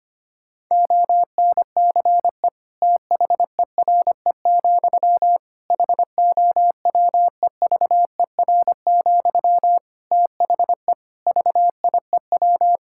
Morse code, heard as a keyed tone, about 25 wpm, 700 Hz. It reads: ONCE THERE, HOWEVER, THE VIEW
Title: Sentences from Top 1000 Words - Review Part 6 - 25wpm